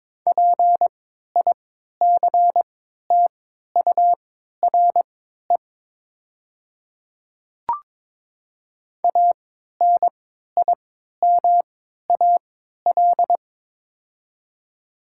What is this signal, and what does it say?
Morse code, heard as a keyed tone, about 22 wpm, spaced as for 13 wpm, 700 Hz. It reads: PICTURE ANIMAL
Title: 3x Character Spacing - Top 200 Words - 22wpm